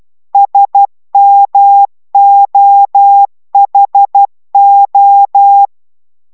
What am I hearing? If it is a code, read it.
Morse code, heard as a keyed tone, about 12 wpm, 800 Hz. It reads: SMOHO